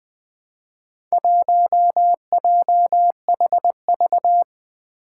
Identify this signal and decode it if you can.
Morse code, heard as a keyed tone, about 20 wpm, 700 Hz. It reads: 1JHV
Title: Four Challenging Letter-Number Combinations 20wpm